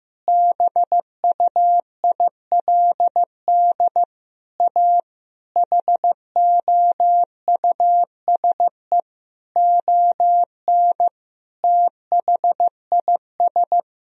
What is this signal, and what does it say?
Morse code, heard as a keyed tone, about 16 wpm, 700 Hz. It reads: BUILD A HOUSE ON THIS
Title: Sentences from Top 1000 Words - Review Part 4 - 15wpm